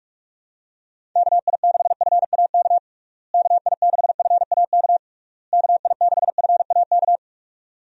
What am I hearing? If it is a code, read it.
Morse code, heard as a keyed tone, about 41 wpm, 700 Hz. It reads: KI6FAK KI6FAK KI6FAK
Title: USA Call Signs - 6 Characters - Repeated 3x - 45wpm